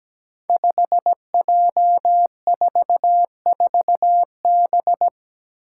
Morse code, heard as a keyed tone, about 17 wpm, 700 Hz. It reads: 5J44B
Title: Five Challenging Letter-Number Combinations 17wpm